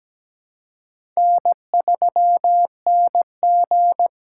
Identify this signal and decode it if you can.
Morse code, heard as a keyed tone, about 17 wpm, 700 Hz. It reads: N3NG
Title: USA Call Signs - 4 Characters - 2x Character Spacing - 17wpm